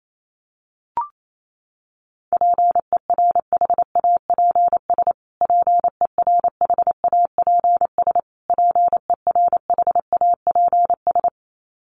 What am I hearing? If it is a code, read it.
Morse code, heard as a keyed tone, about 28 wpm, 700 Hz. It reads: PERHAPS PERHAPS PERHAPS
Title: Top 500 Words Repeated 28wpm